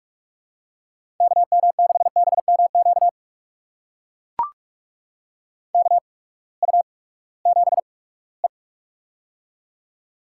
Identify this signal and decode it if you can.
Morse code, heard as a keyed tone, about 44 wpm, spaced as for 22 wpm, 700 Hz. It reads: KM6BMQ KU7E
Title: USA Call Signs - 8x Character Spacing - 45wpm